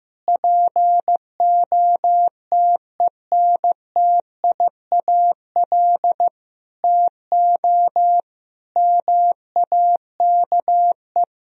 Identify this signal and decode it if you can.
Morse code, heard as a keyed tone, about 15 wpm, 700 Hz. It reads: POTENTIAL TO MAKE
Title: Sentences from Top 1000 Words - Review Part 4 - 15wpm